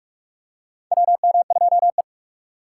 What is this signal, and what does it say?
Morse code, heard as a keyed tone, about 41 wpm, 700 Hz. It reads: WM2E
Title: USA Call Signs - 4 Characters - 3x Character Spacing - 45wpm